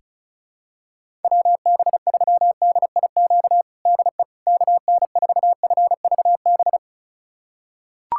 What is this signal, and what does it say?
Morse code, heard as a keyed tone, about 34 wpm, 700 Hz. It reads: WB3DIQ DE KN4FVB
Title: USA Callsign DE Callsign - 6 Characters - 35wpm